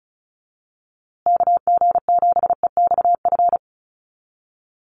Morse code, heard as a keyed tone, about 34 wpm, 700 Hz. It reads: KG7EXF
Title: USA Call Signs Repeated 3x 35wpm